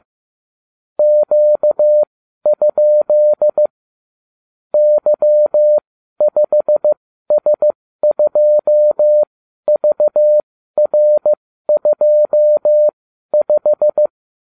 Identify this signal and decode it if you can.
Morse code, heard as a keyed tone, about 15 wpm, spaced as for 12 wpm, 600 Hz. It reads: Q? Y5S2VR25